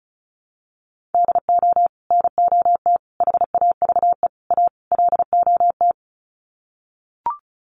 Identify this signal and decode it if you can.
Morse code, heard as a keyed tone, about 34 wpm, 700 Hz. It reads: DO NOT HAVE A LOT
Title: Sets of 5 Words Encore 35wpm